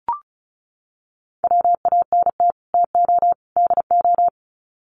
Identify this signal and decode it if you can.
Morse code, heard as a keyed tone, about 34 wpm, 700 Hz. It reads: WANT TO DO
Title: Sets of 3 Words 35wpm